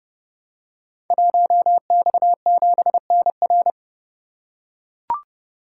Morse code, heard as a keyed tone, about 30 wpm, 700 Hz. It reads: 1X7NR
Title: Code Groups - 5 Letter:Numbers - 30wpm